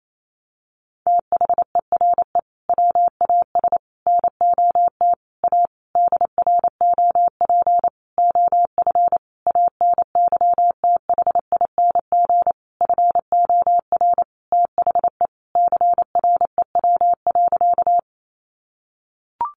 Morse code, heard as a keyed tone, about 28 wpm, 700 Hz. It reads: THERE WAS NOT A DROP OF ANYTHING FOR THE CREW.
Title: Sentences from Top 1500 Words 28wpm